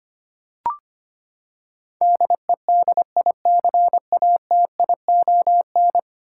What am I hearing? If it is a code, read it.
Morse code, heard as a keyed tone, about 25 wpm, 700 Hz. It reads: DEDICATION
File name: Ten Letter Words - 25wpm